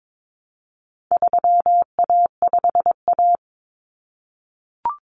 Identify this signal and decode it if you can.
Morse code, heard as a keyed tone, about 22 wpm, 700 Hz. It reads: 3A5A
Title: Common Cut Numbers - 4-Digit 22wpm